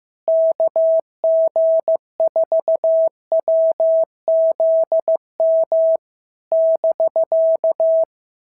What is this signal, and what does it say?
Morse code, heard as a keyed tone, about 15 wpm, 650 Hz. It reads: KG4WZM <BK>
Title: SOTA QSO Practice 15wpm